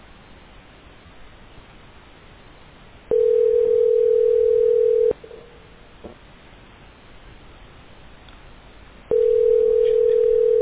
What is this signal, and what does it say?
Ringback tone (North America), 440 + 480 Hz.